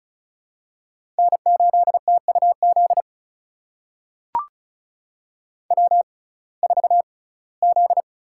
Morse code, heard as a keyed tone, about 36 wpm, spaced as for 21 wpm, 700 Hz. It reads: N8TUZ W4Z
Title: USA Call Signs - 5 Characters - 6x Character Spacing - 35wpm